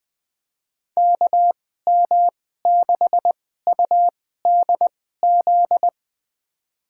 Morse code, heard as a keyed tone, about 20 wpm, spaced as for 14 wpm, 700 Hz. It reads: KM6UDZ
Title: USA Call Signs - 6 Characters - 2x Character Spacing - 20wpm